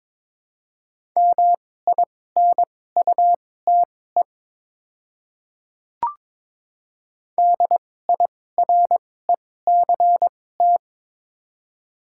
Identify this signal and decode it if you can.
Morse code, heard as a keyed tone, about 22 wpm, spaced as for 16 wpm, 700 Hz. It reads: MINUTE DIRECT
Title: 2x Character Spacing - Top 500 Words - 22wpm